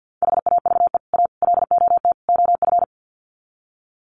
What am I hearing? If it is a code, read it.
Morse code, heard as a keyed tone, about 50 wpm, 700 Hz. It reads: HAVE A LOT OF
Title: Sets of 4 Words 50wpm